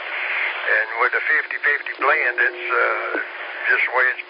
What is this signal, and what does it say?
Emergency Alert System (SAME) tones, heard as harsh buzzing bursts, with speech. It is heard from 0.8 to 3.9 s.